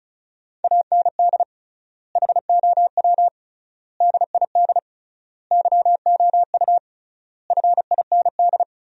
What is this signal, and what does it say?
Morse code, heard as a keyed tone, about 36 wpm, 700 Hz. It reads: AND HOW DID YOU FIND
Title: 3x Word-Spacing - Sentences from Top 100 Words - 35wpm